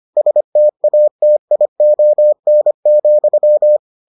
Morse code, heard as a keyed tone, about 25 wpm, 600 Hz. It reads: STATION,